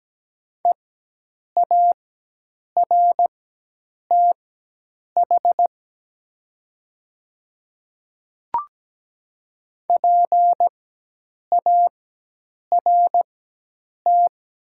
Morse code, heard as a keyed tone, about 17 wpm, spaced as for 8 wpm, 700 Hz. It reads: EARTH PART
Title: 4x Character Spacing - Top 200 Words - 17wpm